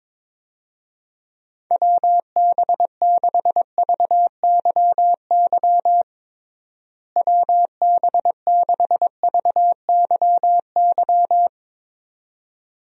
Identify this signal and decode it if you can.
Morse code, heard as a keyed tone, about 21 wpm, 700 Hz. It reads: WB6VYY WB6VYY
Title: USA Call Signs - 6 Characters - Repeated 2x - 22wpm